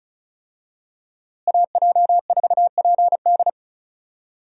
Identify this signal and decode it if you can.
Morse code, heard as a keyed tone, about 34 wpm, 700 Hz. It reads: AJ4PD
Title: USA Call Signs - 5 Characters - Repeated 2x - 35wpm